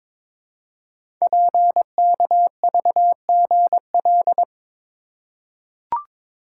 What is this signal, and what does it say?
Morse code, heard as a keyed tone, about 22 wpm, 700 Hz. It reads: PKVGL
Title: Code Groups - 5 Letters - 22wpm